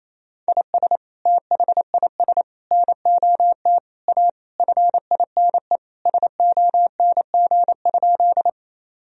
Morse code, heard as a keyed tone, about 27 wpm, 700 Hz. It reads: IS THIS NOT A FINE SONG?
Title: Sentences from Top 1000 Words - Review Part 2 - 28wpm